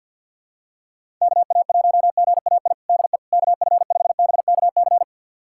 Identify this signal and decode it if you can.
Morse code, heard as a keyed tone, about 50 wpm, 700 Hz. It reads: KA1GAI DE KF5BKC
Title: USA Callsign DE Callsign - 6 Characters - 50wpm